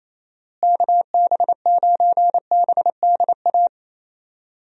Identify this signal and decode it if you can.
Morse code, heard as a keyed tone, about 28 wpm, 700 Hz. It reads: KB9BDA
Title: USA Call Signs - 6 Characters - Repeated 2x - 28wpm